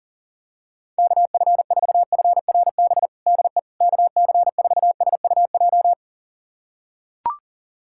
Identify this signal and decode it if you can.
Morse code, heard as a keyed tone, about 40 wpm, 700 Hz. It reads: KF4FRB DE KC4SUJ
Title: USA Callsign DE Callsign - 6 Characters - 40wpm